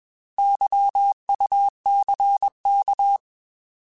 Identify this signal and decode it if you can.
Morse code, heard as a keyed tone, about 21 wpm, 800 Hz. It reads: YUCK
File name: Alphabetical / Alphabetical mp3s / Ys @20